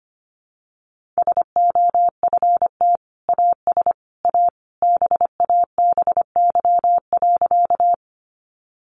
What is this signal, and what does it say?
Morse code, heard as a keyed tone, about 25 wpm, 700 Hz. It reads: SOFT AS A BABY.
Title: Sentences from Top 1500 Words 25wpm